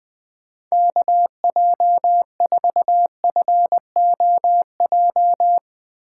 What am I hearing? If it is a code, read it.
Morse code, heard as a keyed tone, about 19 wpm, 700 Hz. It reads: KJ4FOJ
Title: Speed Racing - USA 2x3 Call Signs 17wpm